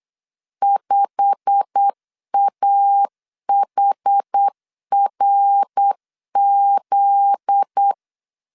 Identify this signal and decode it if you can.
Morse code, heard as a keyed tone, about 8 wpm, 800 Hz. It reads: 5AHRZ
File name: LEZIONI / 8 YJ45 / 8 YJ45 8wpm 3214